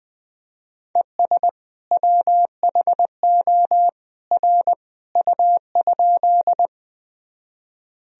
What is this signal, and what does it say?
Morse code, heard as a keyed tone, about 20 wpm, 700 Hz. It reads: ES WHO R U?
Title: Sentences from Top 100 Words and Abbreviations 20wpm